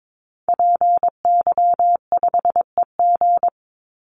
Morse code, heard as a keyed tone, about 22 wpm, 700 Hz. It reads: PY5EG
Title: Call Signs Repeated 2x - 22wpm